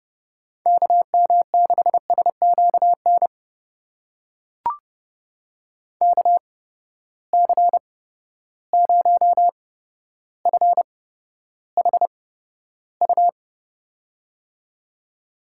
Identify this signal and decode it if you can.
Morse code, heard as a keyed tone, about 30 wpm, spaced as for 13 wpm, 700 Hz. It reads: KM6SQN KC0FHU
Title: USA Call Signs - 6 Characters - 8x Character Spacing - 30wpm